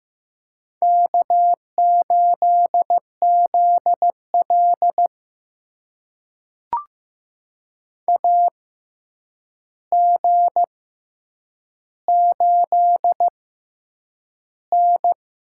Morse code, heard as a keyed tone, about 15 wpm, spaced as for 8 wpm, 700 Hz. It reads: K8ZL AG8N
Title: USA Call Signs - 4 Characters - 6x Character Spacing - 15wpm